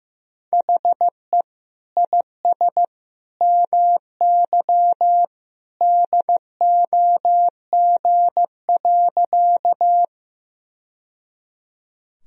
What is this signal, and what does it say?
Morse code, heard as a keyed tone, about 16 wpm, 700 Hz. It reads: HE IS MY DOG.
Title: Sentences from Top 1000 Words - Review Part 2 - 15wpm